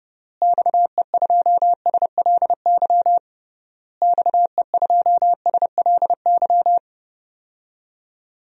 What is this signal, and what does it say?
Morse code, heard as a keyed tone, about 32 wpm, 700 Hz. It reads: XE2SLY XE2SLY
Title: DXCC-100 Call Signs - 6 Characters - Repeated 3x - 30wpm